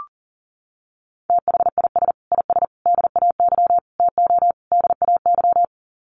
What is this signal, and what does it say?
Morse code, heard as a keyed tone, about 37 wpm, 700 Hz. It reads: THIS IS DAY TO DAY
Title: Sets of 5 Words Encore 40wpm